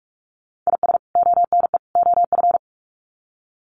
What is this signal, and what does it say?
Morse code, heard as a keyed tone, about 44 wpm, 700 Hz. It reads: IS ONE OF
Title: Sets of 3 Words 45wpm